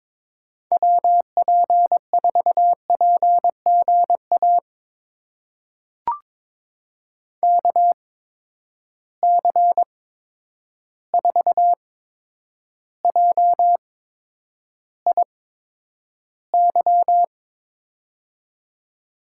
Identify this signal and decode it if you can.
Morse code, heard as a keyed tone, about 22 wpm, spaced as for 9 wpm, 700 Hz. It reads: WP4PGA KC4JIY
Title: USA Call Signs - 6 Characters - 8x Character Spacing - 22wpm